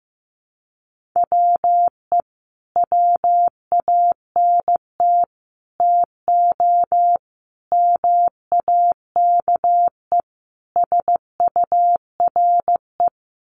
Morse code, heard as a keyed tone, about 15 wpm, 700 Hz. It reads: WE WANT TO MAKE SURE
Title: Sets of 5 Words Encore2 15wpm